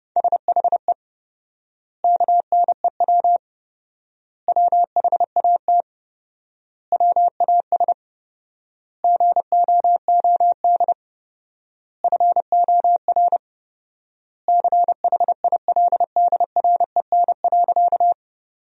Morse code, heard as a keyed tone, about 30 wpm, 700 Hz. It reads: SHE KNEW WHAT WAS GOOD FOR CHILDREN.
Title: 4x Word-Spacing - Sentences from Top 400 Words - 30wpm